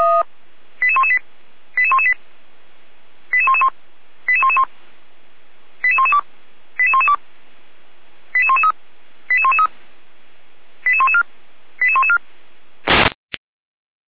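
Five-tone selective call (ZVEI), heard as quick runs of five tones, tones of about 70 ms. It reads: 88118 88118 88111 88111 88112 88112 88113 88113 88114 88114
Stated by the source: Five-tone ZVEI selective call 88497, received by SDR, feuerwehr-orlamuende fire brigade.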